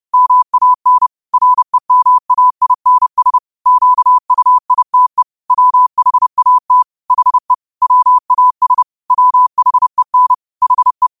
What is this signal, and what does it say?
Morse code, heard as a keyed tone, about 30 wpm, 1000 Hz. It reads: MAN REMAINS QUITE WHAT HE WAS WHEN HE
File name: 2026-02-23-30wpm